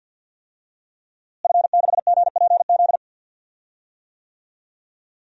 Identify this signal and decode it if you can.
Morse code, heard as a keyed tone, about 52 wpm, 700 Hz. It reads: F6GPZ
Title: DXCC-100 Call Signs - 5 Characters - Repeated 3x - 50wpm